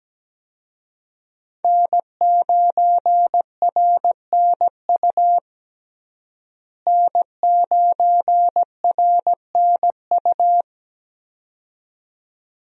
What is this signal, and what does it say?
Morse code, heard as a keyed tone, about 17 wpm, 700 Hz. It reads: N9RNU N9RNU
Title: USA Call Signs - 5 Characters - Repeated 2x - 17wpm